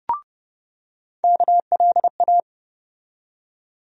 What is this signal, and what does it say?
Morse code, heard as a keyed tone, about 30 wpm, 700 Hz. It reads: KLA
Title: USA Call Sign Postfixes - 3 Letters 30wpm